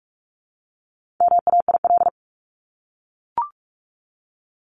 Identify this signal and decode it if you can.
Morse code, heard as a keyed tone, about 44 wpm, 700 Hz. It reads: MAIL
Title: Top 1751-2000 Words 45wpm